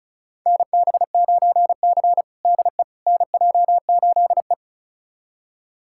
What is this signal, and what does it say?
Morse code, heard as a keyed tone, about 34 wpm, 690 Hz. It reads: NB9C DE NJ8E
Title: USA Callsign DE Callsign - 4 Characters - 35wpm